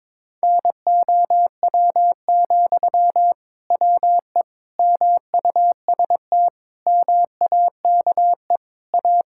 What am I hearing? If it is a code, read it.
Morse code, heard as a keyed tone, about 22 wpm, 700 Hz. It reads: NOW, WE MUST MAKE A
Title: Sentences from Top 1000 Words - Review Part 6 - 22wpm